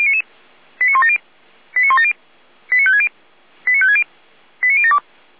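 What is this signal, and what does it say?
Five-tone selective call (ZVEI), heard as quick runs of five tones, tones of about 70 ms. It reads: 89170 89170 89560 89560 89072
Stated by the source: ZVEI five-tone selective call received by SDR.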